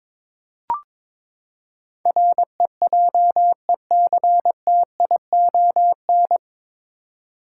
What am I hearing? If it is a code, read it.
Morse code, heard as a keyed tone, about 21 wpm, 700 Hz. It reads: REJECTION
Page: Morse Code Ninja practice 22wpm